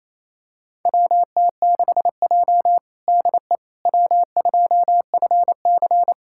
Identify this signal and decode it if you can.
Morse code, heard as a keyed tone, about 27 wpm, 700 Hz. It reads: WT6J DE W2FC
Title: USA Callsign DE Callsign - 4 Characters - 28wpm